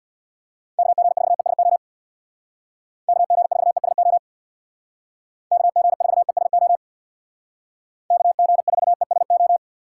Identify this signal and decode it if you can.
Morse code, heard as a keyed tone, about 65 wpm, 700 Hz. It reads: KG4ESO KG4ESO KG4ESO KG4ESO